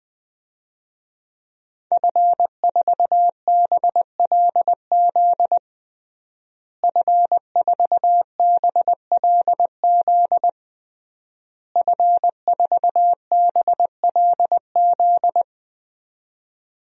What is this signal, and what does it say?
Morse code, heard as a keyed tone, about 20 wpm, 700 Hz. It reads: F4BLZ F4BLZ F4BLZ
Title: DXCC-100 Call Signs - 5 Characters - Repeated 3x - 20wpm